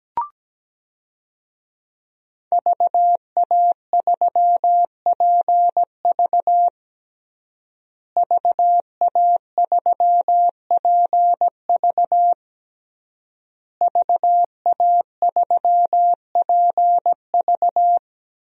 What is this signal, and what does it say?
Morse code, heard as a keyed tone, about 17 wpm, 700 Hz. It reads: VA3PV VA3PV VA3PV
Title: DXCC-100 Call Signs - 5 Characters - Repeated 3x - 17wpm